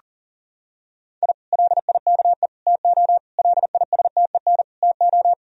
Morse code, heard as a keyed tone, about 37 wpm, 700 Hz. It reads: I LIKE TO LISTEN TO
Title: Sentences from Top 1000 Words - Review Part 3 - 40wpm